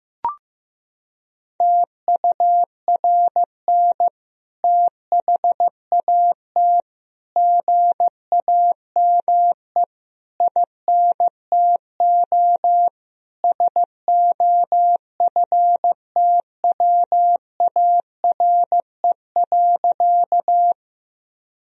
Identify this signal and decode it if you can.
Morse code, heard as a keyed tone, about 15 wpm, 700 Hz. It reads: TURN THAT GAME INTO SOFTWARE.
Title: Sentences from Top 1000 Words - Review Part 3 - 15wpm